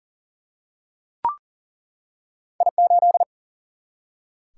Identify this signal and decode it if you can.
Morse code, heard as a keyed tone, about 39 wpm, 700 Hz. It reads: I8
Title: Single-Letter Single-Number 40wpm